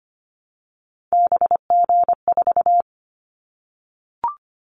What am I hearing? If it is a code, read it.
Morse code, heard as a keyed tone, about 25 wpm, 700 Hz. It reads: BG4
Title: Callsign Prefixes for Top 100 DXCC - 25wpm